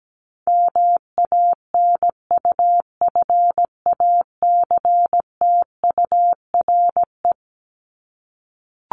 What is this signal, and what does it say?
Morse code, heard as a keyed tone, about 17 wpm, 700 Hz. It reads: MANUFACTURE
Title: ING Suffix - Top 100 Words with Base Words 17wpm